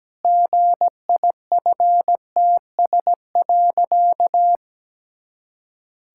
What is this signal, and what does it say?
Morse code, heard as a keyed tone, about 17 wpm, 700 Hz. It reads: GIFTS.
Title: Pangram Challenge v2 - 17wpm